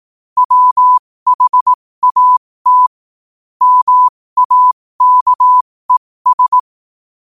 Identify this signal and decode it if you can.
Morse code, heard as a keyed tone, about 18 wpm, spaced as for 15 wpm, 1000 Hz. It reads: WHAT MAKES
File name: qotdfiles / 15wpm